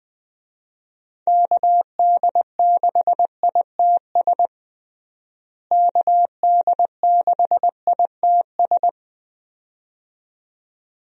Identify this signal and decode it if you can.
Morse code, heard as a keyed tone, about 20 wpm, 700 Hz. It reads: KD6ITS KD6ITS
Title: USA Call Signs - 6 Characters - Repeated 2x - 20wpm